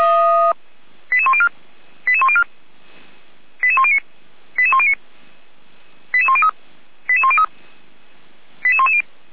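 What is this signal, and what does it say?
Five-tone selective call (ZVEI), heard as quick runs of five tones, tones of about 70 ms. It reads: 88104 88104 88109 88109 88103 88103 88110
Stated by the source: ZVEI five-tone selective call received by SDR, from the Feuerwehr Orlamünde (f85360) archive.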